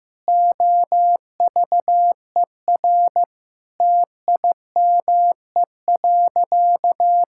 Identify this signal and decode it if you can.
Morse code, heard as a keyed tone, about 15 wpm, 700 Hz. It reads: OVER TIME.
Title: Sentences from Top 1000 Words - Review Part 5 - 15wpm